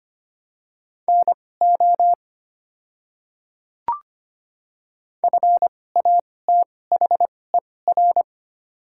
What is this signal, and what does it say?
Morse code, heard as a keyed tone, about 25 wpm, spaced as for 18 wpm, 700 Hz. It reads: NO FATHER
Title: 2x Character Spacing - Top 200 Words - 25wpm